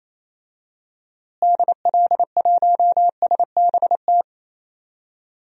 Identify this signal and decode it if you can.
Morse code, heard as a keyed tone, about 27 wpm, 700 Hz. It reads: DL1SBT
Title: DXCC-100 Call Signs - 6 Characters - Repeated 3x - 28wpm